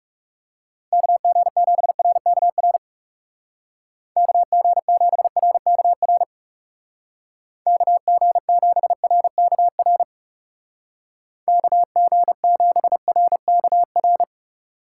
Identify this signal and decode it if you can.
Morse code, heard as a keyed tone, about 40 wpm, 700 Hz. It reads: KG7RKR KG7RKR KG7RKR KG7RKR